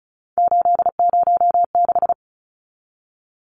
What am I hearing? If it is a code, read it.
Morse code, heard as a keyed tone, about 34 wpm, 700 Hz. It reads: 806
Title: Numbers - 3-Digit 35wpm